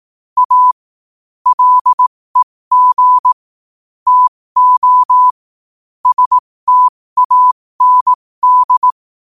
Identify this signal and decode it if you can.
Morse code, heard as a keyed tone, about 18 wpm, spaced as for 15 wpm, 1000 Hz. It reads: A LEG TO STAND
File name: qotdfiles / 15wpm